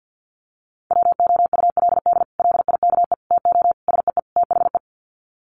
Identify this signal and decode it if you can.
Morse code, heard as a keyed tone, about 50 wpm, 700 Hz. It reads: WOULD LIKE TO SEE THE
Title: Sets of 5 Words Encore2 50wpm